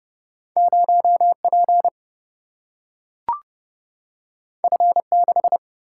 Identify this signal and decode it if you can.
Morse code, heard as a keyed tone, about 31 wpm, 700 Hz. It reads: 0P F6
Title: Single-Letter Single-Number 30wpm